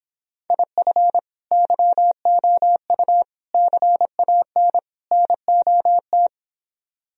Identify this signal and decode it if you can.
Morse code, heard as a keyed tone, about 26 wpm, 700 Hz. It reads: IF YOU CAN NOT